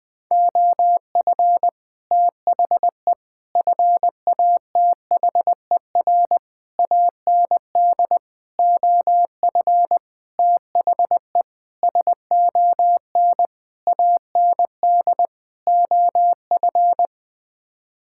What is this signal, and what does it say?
Morse code, heard as a keyed tone, about 19 wpm, 700 Hz. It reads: OF THE FATHER AND OF THE SON AND OF
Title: Sets of 9 Words - 20wpm